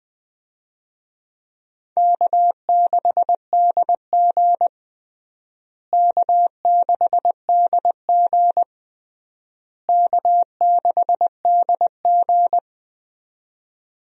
Morse code, heard as a keyed tone, about 21 wpm, 700 Hz. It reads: K6DG K6DG K6DG
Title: USA Call Signs - 4 Characters - Repeated 3x - 20wpm